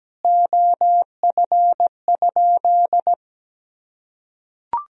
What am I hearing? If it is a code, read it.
Morse code, heard as a keyed tone, about 17 wpm, 700 Hz. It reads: OF?
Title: Sentences from Top 200 Words and Abbreviations 17wpm